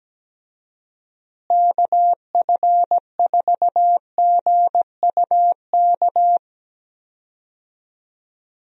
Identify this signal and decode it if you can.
Morse code, heard as a keyed tone, about 17 wpm, 700 Hz. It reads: KF4GUK
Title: USA Call Signs - 6 Characters - Repeated 2x - 17wpm